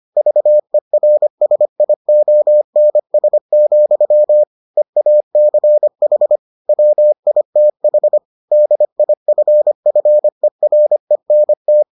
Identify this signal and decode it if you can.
Morse code, heard as a keyed tone, about 25 wpm, 600 Hz. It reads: VERSIONS, EACH WITH DIFFERENT